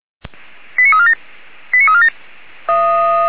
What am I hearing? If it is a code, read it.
Five-tone selective call (ZVEI), heard as quick runs of five tones, tones of about 70 ms. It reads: 89347 89347